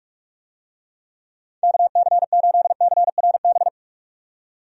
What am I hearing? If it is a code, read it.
Morse code, heard as a keyed tone, about 42 wpm, 700 Hz. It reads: KC8CRB
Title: USA Call Signs - 6 Characters - 45wpm